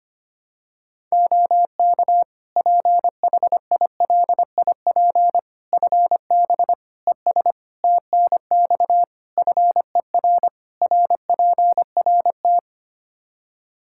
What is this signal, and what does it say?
Morse code, heard as a keyed tone, about 25 wpm, 700 Hz. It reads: OK PHILIP FB ES TNX FER RPRT
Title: K1USN SST QSO Practice 25wpm